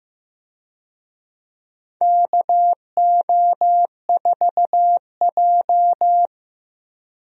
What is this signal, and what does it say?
Morse code, heard as a keyed tone, about 15 wpm, 700 Hz. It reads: KO4J
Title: USA Call Signs - 4 Characters - Repeated 2x - 15wpm